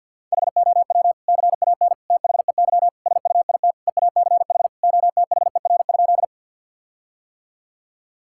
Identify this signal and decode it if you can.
Morse code, heard as a keyed tone, about 50 wpm, 700 Hz. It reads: HOW CAN THEY SUIT EACH OTHER?
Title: Sentences from Top 1600 Words 50wpm